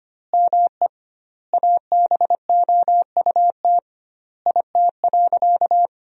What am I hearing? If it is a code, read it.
Morse code, heard as a keyed tone, about 25 wpm, 700 Hz. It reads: ME ABOUT IT.
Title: 2x Word-Spacing - Sentences from Top 200 Words - 25wpm